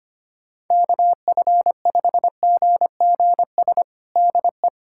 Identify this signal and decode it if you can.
Morse code, heard as a keyed tone, about 25 wpm, 700 Hz. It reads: KF5GGS DE
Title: USA Callsign DE Callsign 25wpm